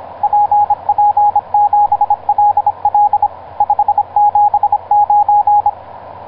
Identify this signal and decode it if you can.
Morse code, heard as a keyed tone, about 26 wpm, 830 Hz, with radio noise. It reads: PP7LL 579
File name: zp6cw ik1rgk etc 12m cw 19feb2010 1640